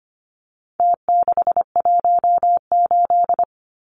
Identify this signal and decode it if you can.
Morse code, heard as a keyed tone, about 25 wpm, 700 Hz. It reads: T618
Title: Numbers - Contest Style - 4-Digit 25wpm